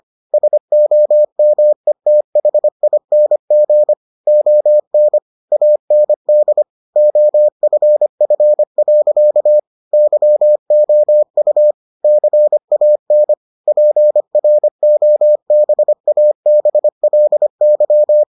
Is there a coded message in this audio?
Yes: Morse code from 0.3 to 18.2 s, heard as a keyed tone, about 25 wpm, 600 Hz. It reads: SOMETHING ON AND OFF. YOU CAN PROBABLY